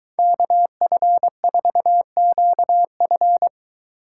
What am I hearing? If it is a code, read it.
Morse code, heard as a keyed tone, about 21 wpm, 700 Hz. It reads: KF4QF